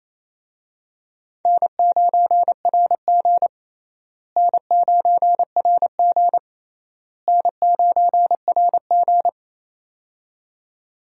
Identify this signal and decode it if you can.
Morse code, heard as a keyed tone, about 28 wpm, 700 Hz. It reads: N9RG N9RG N9RG
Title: USA Call Signs - 4 Characters - Repeated 3x - 28wpm